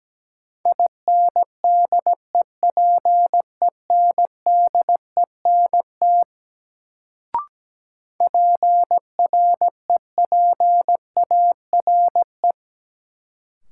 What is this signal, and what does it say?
Morse code, heard as a keyed tone, about 17 wpm, 700 Hz. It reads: INDEPENDENT PREPARE
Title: Sentences from Top 900 Words - New Word Drill - 17wpm